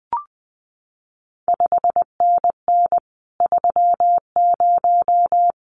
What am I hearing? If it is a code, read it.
Morse code, heard as a keyed tone, about 20 wpm, 700 Hz. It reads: 5NN 30
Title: CQ WW DX Contest Exchange Encore - 20wpm